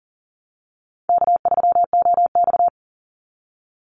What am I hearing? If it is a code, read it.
Morse code, heard as a keyed tone, about 40 wpm, 700 Hz. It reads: K3OX
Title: Easy Call Signs Repeated 2x - 40wpm